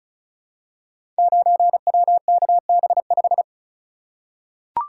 Morse code, heard as a keyed tone, about 36 wpm, 700 Hz. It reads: 9WKB5
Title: Code Groups - 5 Letter:Numbers - 35wpm